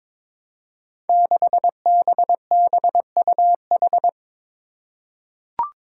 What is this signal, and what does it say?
Morse code, heard as a keyed tone, about 23 wpm, 700 Hz. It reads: 6BBUH
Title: Five Challenging Letter-Number Combinations 22wpm